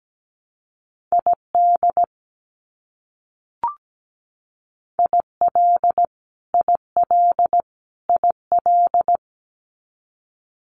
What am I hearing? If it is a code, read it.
Morse code, heard as a keyed tone, about 17 wpm, 700 Hz. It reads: ID IL IL IL
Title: US States Repeated 3x 17wpm